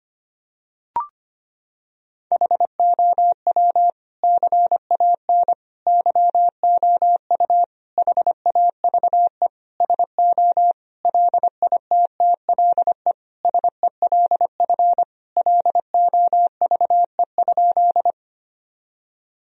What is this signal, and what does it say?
Morse code, heard as a keyed tone, about 25 wpm, 700 Hz. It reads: HOW CAN YOU HAVE SO LITTLE SELF LOVE?
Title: Sentences from Top 1000 Words - Review Part 1 - 25wpm